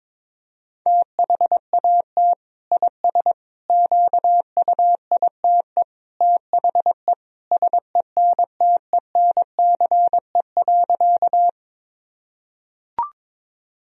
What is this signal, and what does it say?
Morse code, heard as a keyed tone, about 22 wpm, 700 Hz. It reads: THAT IS QUITE THE SENTENCE.
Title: Sentences from Top 1000 Words - Review Part 1 - 22wpm